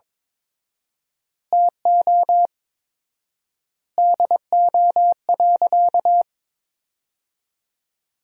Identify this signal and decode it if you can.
Morse code, heard as a keyed tone, about 22 wpm, 700 Hz. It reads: TO DO.
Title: 4x Word-Spacing - Sentences from Top 100 Words - 22wpm